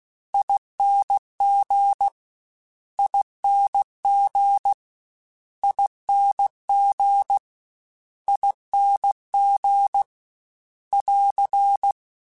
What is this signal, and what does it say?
Morse code, heard as a keyed tone, about 16 wpm, 800 Hz. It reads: ING ING ING ING <AR>
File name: General / General-mp3s / Suffixes @15